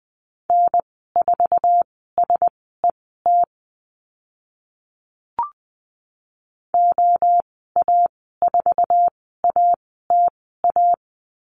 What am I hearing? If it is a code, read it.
Morse code, heard as a keyed tone, about 20 wpm, spaced as for 14 wpm, 700 Hz. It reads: N4SET OA4ATA
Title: 076 Lesson 13g - Letter L - Call Signs 20wpm